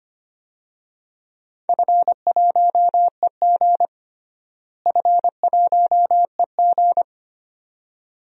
Morse code, heard as a keyed tone, about 25 wpm, 700 Hz. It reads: F1EG F1EG
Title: DXCC-100 Call Signs - 4 Characters - Repeated 2x - 25wpm